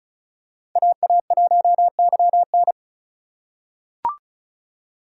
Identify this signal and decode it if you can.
Morse code, heard as a keyed tone, about 34 wpm, 700 Hz. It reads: AA1YN
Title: USA Call Signs - 5 Characters - 4x Character Spacing - 35wpm